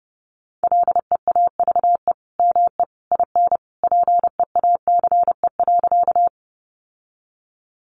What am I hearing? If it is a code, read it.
Morse code, heard as a keyed tone, about 30 wpm, 700 Hz. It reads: LEAVE ME IN PEACE.
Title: Sentences from Top 1500 Words 30wpm